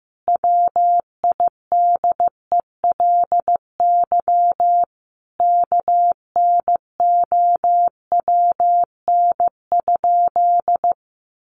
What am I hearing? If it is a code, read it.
Morse code, heard as a keyed tone, about 15 wpm, 700 Hz. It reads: WIDELY KNOWN?
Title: Sentences from Top 1200 Words 15wpm